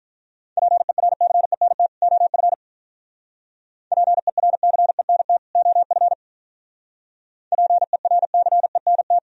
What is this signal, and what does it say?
Morse code, heard as a keyed tone, about 45 wpm, 700 Hz. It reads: PERCENT OF PERCENT OF PERCENT